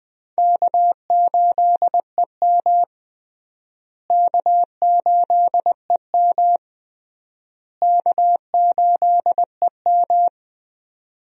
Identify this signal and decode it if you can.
Morse code, heard as a keyed tone, about 20 wpm, 700 Hz. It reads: K8EM K8EM K8EM
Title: USA Call Signs - 4 Characters - Repeated 3x - 20wpm